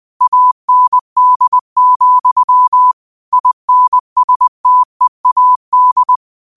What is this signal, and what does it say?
Morse code, heard as a keyed tone, about 20 wpm, 1000 Hz. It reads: AND, INSTEAD
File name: qotdfiles / 20wpm